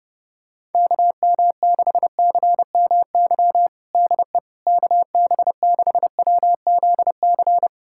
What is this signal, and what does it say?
Morse code, heard as a keyed tone, about 30 wpm, 700 Hz. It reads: KM6CMY DE KB6WZC
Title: USA Callsign DE Callsign - 6 Characters - 30wpm